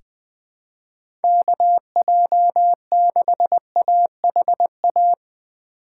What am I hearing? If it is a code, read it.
Morse code, heard as a keyed tone, about 20 wpm, 700 Hz. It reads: KJ6AHA
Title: USA Call Signs - 6 Characters - 4x Character Spacing - 20wpm